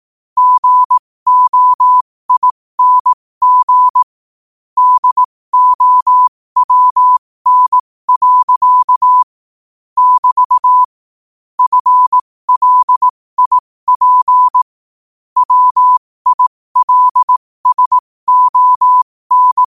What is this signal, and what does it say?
Morse code, heard as a keyed tone, about 18 wpm, spaced as for 15 wpm, 1000 Hz. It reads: GOING DOWN. <BT> FLIP WILSON
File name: qotdfiles / 15wpm